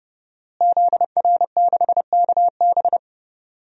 Morse code, heard as a keyed tone, about 30 wpm, 700 Hz. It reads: ZR6KB
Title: DXCC-100 Call Signs - 5 Characters - Repeated 2x - 30wpm